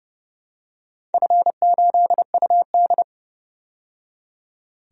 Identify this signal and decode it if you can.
Morse code, heard as a keyed tone, about 30 wpm, 700 Hz. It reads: F8UD
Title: DXCC-100 Call Signs - 4 Characters - 30wpm